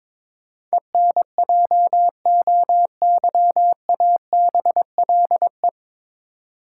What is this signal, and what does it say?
Morse code, heard as a keyed tone, about 22 wpm, 700 Hz. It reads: ENJOYABLE
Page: Morse Code Ninja practice 22wpm